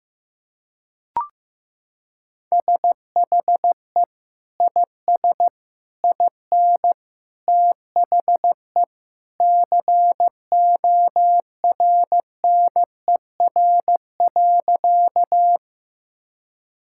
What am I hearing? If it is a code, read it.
Morse code, heard as a keyed tone, about 15 wpm, 700 Hz. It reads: SHE IS IN THE CORNER.
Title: Sentences from Top 1600 Words 15wpm